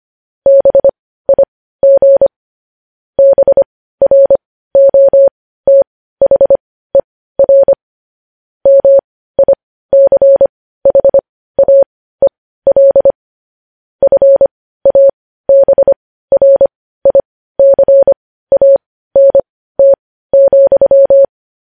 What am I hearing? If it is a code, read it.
Morse code, heard as a keyed tone, about 25 wpm, spaced as for 15 wpm, 550 Hz. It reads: BIG BROTHER MICHAEL FABRICANT,
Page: news-headlines-in-morse-code-at-15-wpm